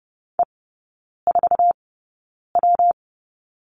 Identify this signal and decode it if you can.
Morse code, heard as a keyed tone, about 30 wpm, spaced as for 9 wpm, 700 Hz. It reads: E4W
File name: Lesson 19e - Letter w - Three Characters 30wpm